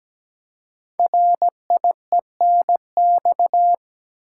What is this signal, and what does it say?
Morse code, heard as a keyed tone, about 17 wpm, 700 Hz. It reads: RIENX
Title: Code Groups - 5 Letters - 17wpm